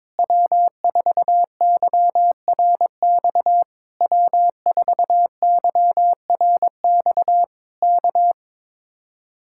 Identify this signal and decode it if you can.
Morse code, heard as a keyed tone, about 22 wpm, 700 Hz. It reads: W4YRX W4YRX K
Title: Speed-Racing Common QSO Phrases - 22wpm